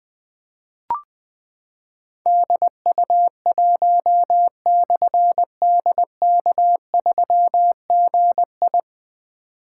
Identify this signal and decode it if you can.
Morse code, heard as a keyed tone, about 20 wpm, 700 Hz. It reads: DU1/DK3GI
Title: Difficult Call Signs 20wpm